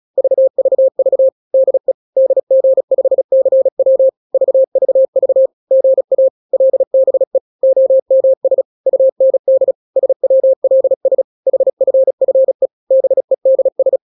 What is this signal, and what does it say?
Morse code, heard as a keyed tone, about 36 wpm, 530 Hz. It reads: VVV DE DG5CW VVV GA LBE OMS UND SWLS HFFE BEDS
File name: daily / 35wpm